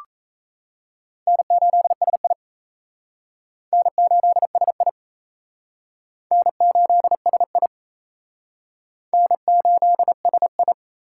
Morse code, heard as a keyed tone, about 35 wpm, 700 Hz. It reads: N8SI N8SI N8SI N8SI
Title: Speed Racing - USA 1x2 Call Signs 28wpm